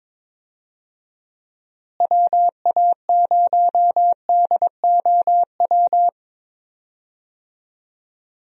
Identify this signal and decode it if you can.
Morse code, heard as a keyed tone, about 22 wpm, 700 Hz. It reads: WA0DOW
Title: USA Call Signs - 6 Characters - Repeated 2x - 22wpm